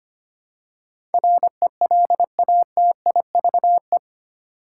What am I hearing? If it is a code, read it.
Morse code, heard as a keyed tone, about 25 wpm, 700 Hz. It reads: RELATIVE
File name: Eight Letter Words - 25wpm